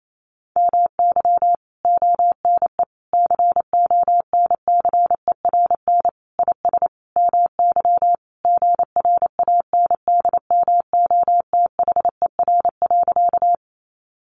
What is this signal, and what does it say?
Morse code, heard as a keyed tone, about 28 wpm, 700 Hz. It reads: MY ONE CONCERN IS MY GRANDMOTHER.
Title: Sentences from Top 1200 Words 28wpm